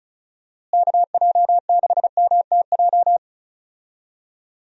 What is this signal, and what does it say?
Morse code, heard as a keyed tone, about 36 wpm, 700 Hz. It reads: KJ6MTJ
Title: USA Call Signs - 6 Characters - Repeated 3x - 35wpm